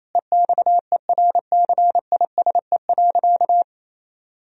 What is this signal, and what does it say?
Morse code, heard as a keyed tone, about 28 wpm, 700 Hz. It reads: EXERCISE.
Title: Sentences from Top 1000 Words - Review Part 4 - 28wpm